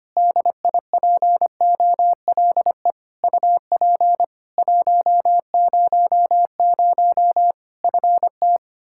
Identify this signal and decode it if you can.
Morse code, heard as a keyed tone, about 25 wpm, 700 Hz. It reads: DIPOLE UP 100 FT
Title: Practice QSO - Exchange 2 - Info - 25wpm